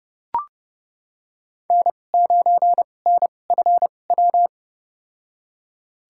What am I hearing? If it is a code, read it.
Morse code, heard as a keyed tone, about 32 wpm, spaced as for 22 wpm, 700 Hz. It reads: N9NFW
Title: USA Call Signs - 5 Characters - 2x Character Spacing - 30wpm